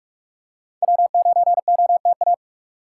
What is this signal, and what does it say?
Morse code, heard as a keyed tone, about 46 wpm, 700 Hz. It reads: W9OTA
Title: USA Call Signs - 5 Characters - 4x Character Spacing - 45wpm